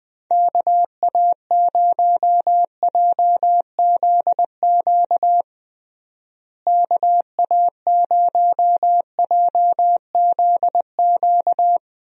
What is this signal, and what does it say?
Morse code, heard as a keyed tone, about 20 wpm, 700 Hz. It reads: KA0JZQ KA0JZQ
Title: USA Call Signs - 6 Characters - Repeated 3x - 20wpm